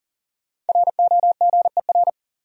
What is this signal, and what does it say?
Morse code, heard as a keyed tone, about 40 wpm, 700 Hz. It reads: ROGER
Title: USA Common Names - Max 5 Letters - 40wpm